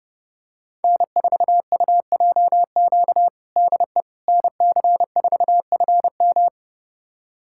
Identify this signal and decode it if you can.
Morse code, heard as a keyed tone, about 30 wpm, 700 Hz. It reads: N4UJQ DE NC4FM
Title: USA Callsign DE Callsign - 5 Characters - 30wpm